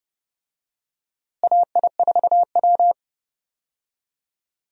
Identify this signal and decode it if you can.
Morse code, heard as a keyed tone, about 29 wpm, 700 Hz. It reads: AI4W
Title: USA Call Signs - 4 Characters - Repeated 2x - 30wpm